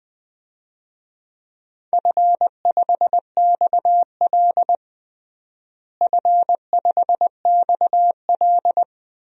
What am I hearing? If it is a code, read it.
Morse code, heard as a keyed tone, about 20 wpm, 700 Hz. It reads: F5XL F5XL
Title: DXCC-100 Call Signs - 4 Characters - Repeated 3x - 20wpm